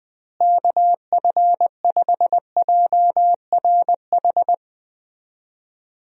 Morse code, heard as a keyed tone, about 20 wpm, 700 Hz. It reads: KF5JRH
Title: USA Call Signs - 6 Characters - Repeated 2x - 20wpm